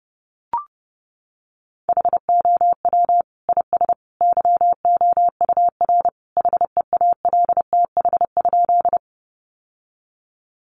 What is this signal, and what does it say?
Morse code, heard as a keyed tone, about 30 wpm, 700 Hz. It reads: HOW IS YOUR HEALTH?
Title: Sentences from Top 600 Words 30wpm